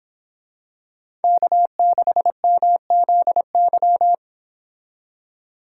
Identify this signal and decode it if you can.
Morse code, heard as a keyed tone, about 26 wpm, 700 Hz. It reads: K6MZY
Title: Speed Racing - USA 1x3 Call Signs 22wpm